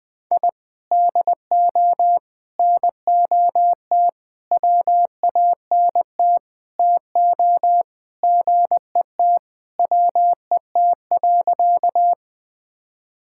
Morse code, heard as a keyed tone, about 20 wpm, 700 Hz. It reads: I DO NOT WANT TO GET WET.
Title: Sentences with 5-Word Phrases - 20wpm